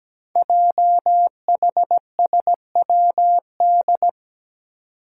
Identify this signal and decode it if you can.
Morse code, heard as a keyed tone, about 17 wpm, 700 Hz. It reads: JHSWD
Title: Code Groups - 5 Letters - 17wpm